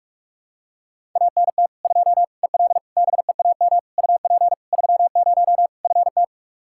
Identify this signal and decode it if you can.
Morse code, heard as a keyed tone, about 41 wpm, 700 Hz. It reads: ANT 2 EL BEAM UP 30 FT
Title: K1USN SST QSO Practice 45wpm